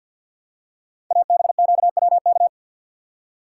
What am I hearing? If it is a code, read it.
Morse code, heard as a keyed tone, about 50 wpm, 690 Hz. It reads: ABQWK
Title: Code Groups - 5 Letters - 50wpm